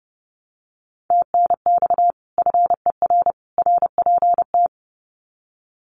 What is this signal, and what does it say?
Morse code, heard as a keyed tone, about 30 wpm, 700 Hz. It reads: TNX FER RPT
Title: Common QSO Phrases 30wpm